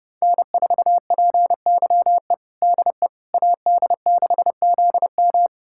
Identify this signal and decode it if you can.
Morse code, heard as a keyed tone, about 31 wpm, 700 Hz. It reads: N4PYE DE AD6ZM
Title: USA Callsign DE Callsign - 5 Characters - 30wpm